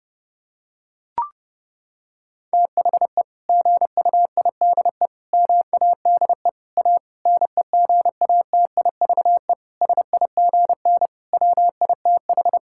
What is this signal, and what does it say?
Morse code, heard as a keyed tone, about 32 wpm, 700 Hz. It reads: THE GUIDE MADE A NEGATIVE SIGN WITH
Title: Sentences from Top 1000 Words - Review Part 6 - 30wpm